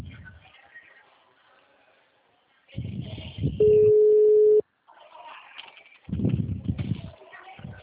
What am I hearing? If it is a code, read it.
Ringback tone (Europe, 425 Hz).